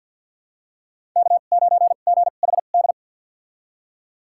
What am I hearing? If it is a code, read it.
Morse code, heard as a keyed tone, about 46 wpm, spaced as for 35 wpm, 700 Hz. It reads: K9GHD
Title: USA Call Signs - 2x Character Spacing - 50wpm